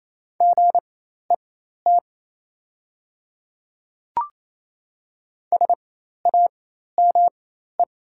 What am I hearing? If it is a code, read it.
Morse code, heard as a keyed tone, about 28 wpm, spaced as for 13 wpm, 690 Hz. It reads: GET SAME
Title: 4x Character Spacing - Top 200 Words - 28wpm